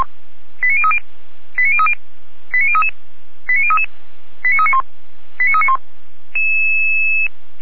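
Five-tone selective call (ZVEI), heard as quick runs of five tones, tones of about 70 ms. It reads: 89030 89030 89033 89033 89391 89391
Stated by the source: ZVEI five-tone selective call 89508 received by SDR, archived by Feuerwehr Orlamünde (fire brigade).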